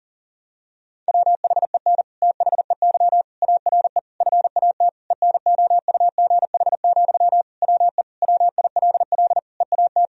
Morse code, heard as a keyed tone, about 40 wpm, 700 Hz. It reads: WHEN THEY ARE FAT ENOUGH, WE WILL EAT
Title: Sentences from Top 1000 Words - Review Part 4 - 40wpm